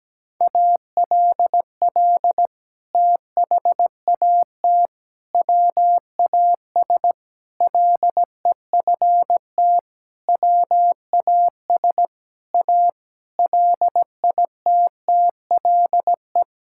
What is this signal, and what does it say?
Morse code, heard as a keyed tone, about 17 wpm, 700 Hz. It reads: ALL THAT WAS LEFT WAS A LITTLE
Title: Sentences from Top 1000 Words - Review Part 5 - 17wpm